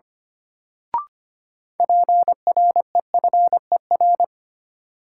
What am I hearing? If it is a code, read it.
Morse code, heard as a keyed tone, about 25 wpm, 700 Hz. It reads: PREFER
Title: Sentences from Top 1100 Words - New Word Drill - 25wpm